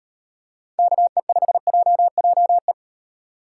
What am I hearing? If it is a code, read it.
Morse code, heard as a keyed tone, about 36 wpm, 700 Hz. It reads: KE5JJE